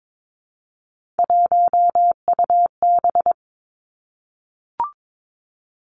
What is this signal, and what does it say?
Morse code, heard as a keyed tone, about 22 wpm, 700 Hz. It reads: 1UB
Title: Three Challenging Letter-Number Combinations 22wpm